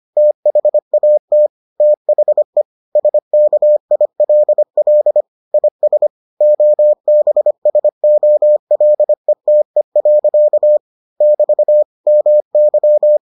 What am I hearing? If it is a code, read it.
Morse code, heard as a keyed tone, about 25 wpm, 600 Hz. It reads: THAT THE SKILL IS OBSOLETE. <BT> MY